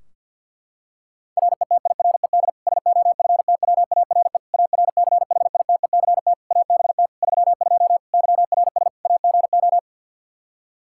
Morse code, heard as a keyed tone, about 51 wpm, 700 Hz. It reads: RETIRED SOFTWARE ARCHITECT ABT 32 YRS AGO
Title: Practice QSO - Exchange 3 - More Info - 50wpm